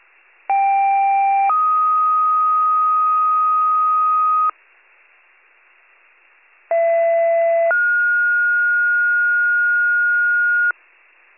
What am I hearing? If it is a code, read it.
Two-tone page, heard as a one-second tone, then a three-second tone, about 790 then 1250 Hz, about 670 then 1430 Hz. It is heard from 0.5 to 10.7 s.